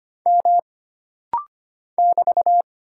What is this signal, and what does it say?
Morse code, heard as a keyed tone, about 25 wpm, 700 Hz. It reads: M <BT>
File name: Single Character (Letter:Number:Punctuation:Prosign) - Rapid-Fire - 25wpm